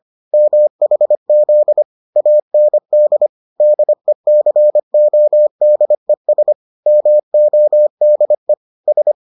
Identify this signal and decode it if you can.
Morse code, heard as a keyed tone, about 25 wpm, 600 Hz. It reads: MHZ AND DECODES MODE S